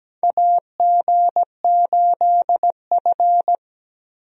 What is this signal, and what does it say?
Morse code, heard as a keyed tone, about 17 wpm, 700 Hz. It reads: AG8F
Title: USA Call Signs - 4 Characters - Repeated 3x - 17wpm